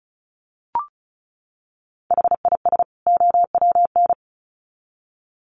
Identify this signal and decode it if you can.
Morse code, heard as a keyed tone, about 34 wpm, 700 Hz. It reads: HIS OWN
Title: Sets of 2 Words 35wpm